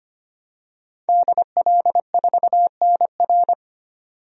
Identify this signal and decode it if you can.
Morse code, heard as a keyed tone, about 25 wpm, 690 Hz. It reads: DL4NR
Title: DXCC-100 Call Signs - 5 Characters - 25wpm